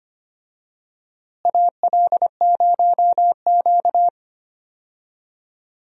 Morse code, heard as a keyed tone, about 25 wpm, 700 Hz. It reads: AL0Q
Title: USA Call Signs - 4 Characters - Repeated 3x - 25wpm